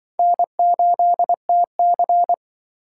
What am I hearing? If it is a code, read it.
Morse code, heard as a keyed tone, about 25 wpm, 700 Hz. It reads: N8TC
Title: Speed Racing - USA 1x2 Call Signs 20wpm